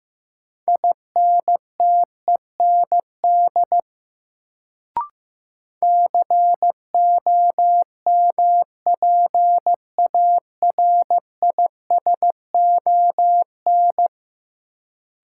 Morse code, heard as a keyed tone, about 15 wpm, 700 Hz. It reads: INTEND COMPARISON
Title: Sentences from Top 1600 Words - New Word Drill Alt - 15wpm